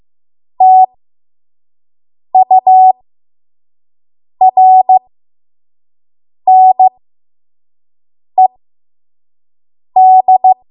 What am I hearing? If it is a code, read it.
Morse code, heard as a keyed tone, about 15 wpm, spaced as for 5 wpm, 750 Hz. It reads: TURNED
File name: Archive / 5 WPM / 05WPM